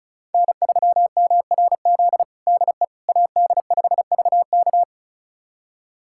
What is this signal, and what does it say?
Morse code, heard as a keyed tone, about 36 wpm, 700 Hz. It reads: N3MRZ DE AD5VK
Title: USA Callsign DE Callsign - 5 Characters - 35wpm